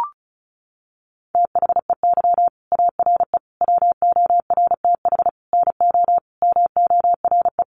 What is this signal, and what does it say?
Morse code, heard as a keyed tone, about 34 wpm, 700 Hz. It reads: THEY ARE WORTH NO MORE
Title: Sentences from Top 800 Words 35wpm